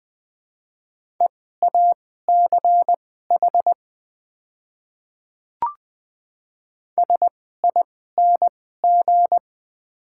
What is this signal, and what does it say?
Morse code, heard as a keyed tone, about 21 wpm, spaced as for 15 wpm, 690 Hz. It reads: EACH SING
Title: 2x Character Spacing - Top 500 Words - 20wpm